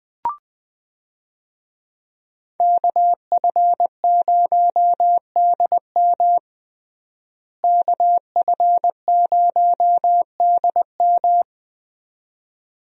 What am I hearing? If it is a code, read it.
Morse code, heard as a keyed tone, about 20 wpm, 700 Hz. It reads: KF0DM KF0DM
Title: USA Call Signs - 5 Characters - Repeated 2x - 20wpm